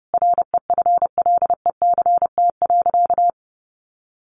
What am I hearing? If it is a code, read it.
Morse code, heard as a keyed tone, about 30 wpm, 700 Hz. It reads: REFLECT.
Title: Sentences from Top 1500 Words 30wpm